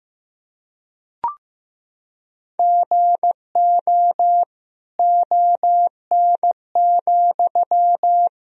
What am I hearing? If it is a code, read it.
Morse code, heard as a keyed tone, about 15 wpm, 700 Hz. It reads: GO ON,
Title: Sentences from Top 100 Words and Abbreviations 15wpm